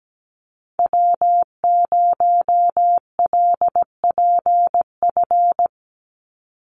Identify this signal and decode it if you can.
Morse code, heard as a keyed tone, about 17 wpm, 700 Hz. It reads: W0LPF
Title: Call Signs Repeated 3x - 17wpm